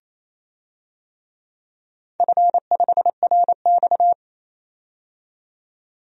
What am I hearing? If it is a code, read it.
Morse code, heard as a keyed tone, about 28 wpm, 700 Hz. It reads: F5RX
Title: DXCC-100 Call Signs - 4 Characters - 28wpm